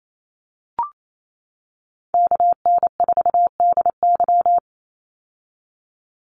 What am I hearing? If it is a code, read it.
Morse code, heard as a keyed tone, about 28 wpm, 700 Hz. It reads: KN4DY
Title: USA Call Signs 28wpm